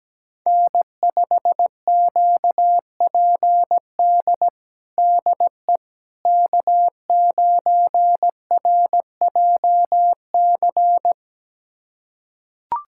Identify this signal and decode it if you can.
Morse code, heard as a keyed tone, about 17 wpm, 700 Hz. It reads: N5QPD DE K9RJC
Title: USA Callsign DE Callsign - 5 Characters - 17wpm